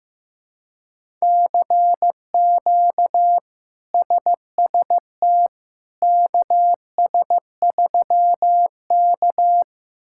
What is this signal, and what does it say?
Morse code, heard as a keyed tone, about 15 wpm, 700 Hz. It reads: CQ SST KS3K
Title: K1USN SST QSO Practice - 15wpm